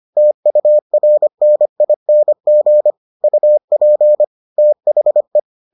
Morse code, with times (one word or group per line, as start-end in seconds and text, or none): 0.2-2.9 TURNING
3.2-4.2 UP
4.6-5.4 THE